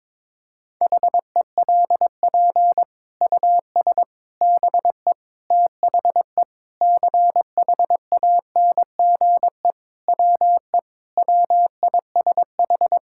Morse code, heard as a keyed tone, about 22 wpm, 700 Hz. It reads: HELP US BE THE CHANGE WE WISH
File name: Sets of 9 Words - 22wpm